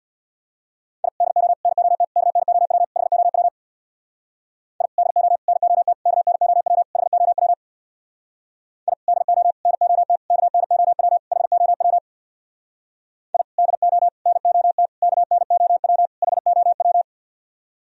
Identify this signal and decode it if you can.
Morse code, heard as a keyed tone, about 65 wpm, 690 Hz. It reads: I DO NOT KNOW HOW I DO NOT KNOW HOW I DO NOT KNOW HOW I DO NOT KNOW HOW